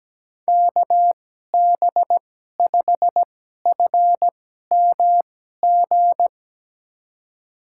Morse code, heard as a keyed tone, about 17 wpm, spaced as for 12 wpm, 700 Hz. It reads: KB5FMG
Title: USA Call Signs - 6 Characters - 2x Character Spacing - 17wpm